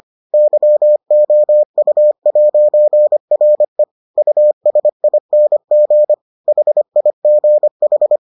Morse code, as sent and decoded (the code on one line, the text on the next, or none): -.-- --- ..- .----. .-. / ..- ... .. -. --. / .... .. --. ....
YOU'RE USING HIGH